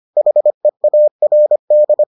Morse code, heard as a keyed tone, about 25 wpm, 600 Hz. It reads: HEARD